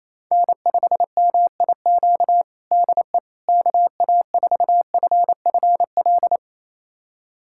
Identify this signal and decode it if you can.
Morse code, heard as a keyed tone, about 28 wpm, 700 Hz. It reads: N5MIQ DE KA4FFL
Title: USA Callsign DE Callsign 28wpm